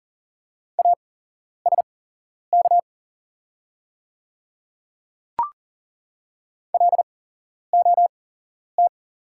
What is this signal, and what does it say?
Morse code, heard as a keyed tone, about 40 wpm, spaced as for 11 wpm, 700 Hz. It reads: ASK LOT
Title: 8x Character Spacing - Top 500 Words - 40wpm